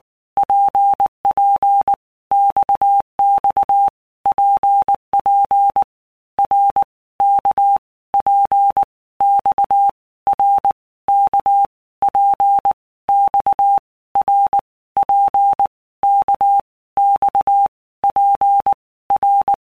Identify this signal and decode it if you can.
Morse code, heard as a keyed tone, about 19 wpm, 800 Hz. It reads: PP XX PP R K P X R K P X R P K X P R
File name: All Groups 1-minute Blocks / Groups@18 / RKPX